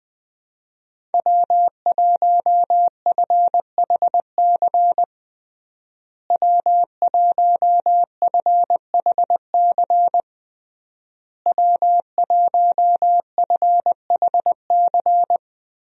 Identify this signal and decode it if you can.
Morse code, heard as a keyed tone, about 21 wpm, 700 Hz. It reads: W1FHC W1FHC W1FHC
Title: USA Call Signs - 5 Characters - Repeated 3x - 20wpm